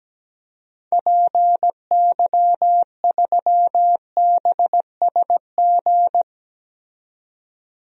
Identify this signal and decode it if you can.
Morse code, heard as a keyed tone, about 17 wpm, 700 Hz. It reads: PY3BSG
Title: DXCC-100 Call Signs - 6 Characters - 17wpm